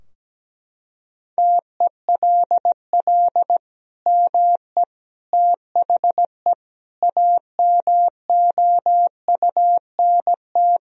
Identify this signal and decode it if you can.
Morse code, heard as a keyed tone, about 17 wpm, 700 Hz. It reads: TELL ME THE AMOUNT
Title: Sentences from Top 1000 Words - Review Part 4 - 17wpm